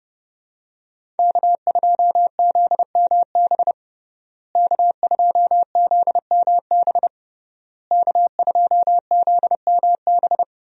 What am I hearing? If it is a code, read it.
Morse code, heard as a keyed tone, about 29 wpm, 700 Hz. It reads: K2ZMB K2ZMB K2ZMB
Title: USA Call Signs - 5 Characters - Repeated 3x - 30wpm